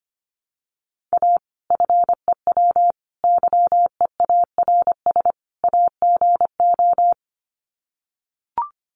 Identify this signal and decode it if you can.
Morse code, heard as a keyed tone, about 25 wpm, 700 Hz. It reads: A FEW YEARS AGO
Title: Sets of 4 Words 25wpm